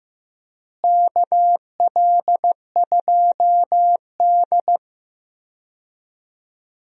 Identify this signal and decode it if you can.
Morse code, heard as a keyed tone, about 15 wpm, 700 Hz. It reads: KL2D
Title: USA Call Signs - 4 Characters - Repeated 2x - 15wpm